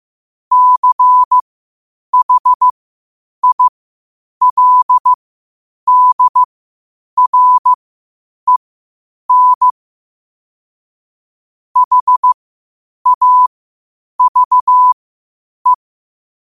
Morse code, heard as a keyed tone, about 15 wpm, spaced as for 8 wpm, 1000 Hz. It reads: CHILDREN HAVE
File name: qotdfiles / 8wpm